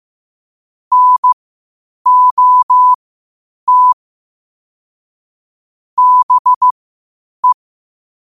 Morse code, heard as a keyed tone, about 15 wpm, spaced as for 8 wpm, 1000 Hz. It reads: NOT BE
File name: qotdfiles / 8wpm